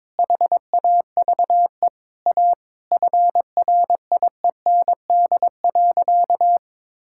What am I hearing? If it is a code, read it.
Morse code, heard as a keyed tone, about 22 wpm, 700 Hz. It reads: HAVE A FRIEND.
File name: practice / 22wpm / Sentences with 3-Word Phrases - 22wpm